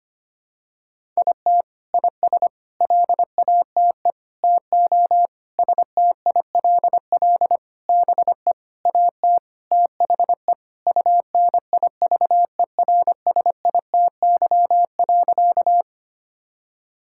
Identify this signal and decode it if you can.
Morse code, heard as a keyed tone, about 25 wpm, 700 Hz. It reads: IT IS LATE TO STILL BE AT THE UNIVERSITY.
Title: Sentences from Top 1000 Words - Review Part 5 - 25wpm